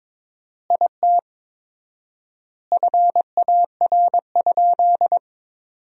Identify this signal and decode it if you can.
Morse code, heard as a keyed tone, about 22 wpm, 700 Hz. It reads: IT FAR?
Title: 4x Word-Spacing - Sentences from Top 300 Words - 22wpm